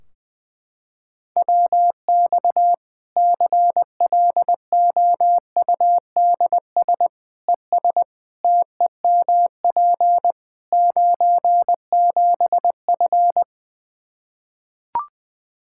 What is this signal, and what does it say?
Morse code, heard as a keyed tone, about 21 wpm, 700 Hz. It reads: WX CLOUDS ES TEMP 97F
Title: Practice QSO - Exchange 2 - Info - 20wpm